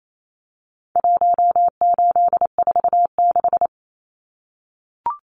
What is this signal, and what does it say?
Morse code, heard as a keyed tone, about 28 wpm, 700 Hz. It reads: 1846
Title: Numbers - Contest Style - 4-Digit 28wpm